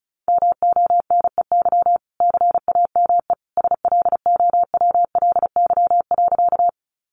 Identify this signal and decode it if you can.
Morse code, heard as a keyed tone, about 34 wpm, 700 Hz. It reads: MONEY CAME SLOWLY.
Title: Sentences from Top 1100 Words 35wpm